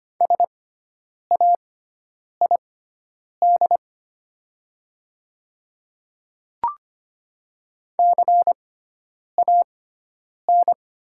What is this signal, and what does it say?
Morse code, heard as a keyed tone, about 25 wpm, spaced as for 9 wpm, 700 Hz. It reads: SAID CAN
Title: 6x Character Spacing - Top 100 Words - 25wpm